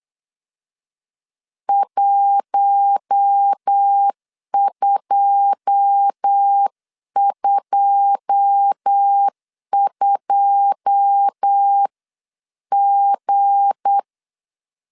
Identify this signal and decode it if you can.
Morse code, heard as a keyed tone, about 8 wpm, 800 Hz. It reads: 1222G
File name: LEZIONI / 9 2 8wpm 3437